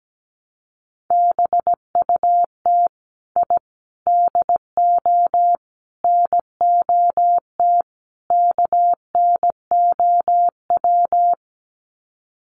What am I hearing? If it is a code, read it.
Morse code, heard as a keyed tone, about 17 wpm, 700 Hz. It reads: BUT I DO NOT KNOW
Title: Sets of 5 Words 17wpm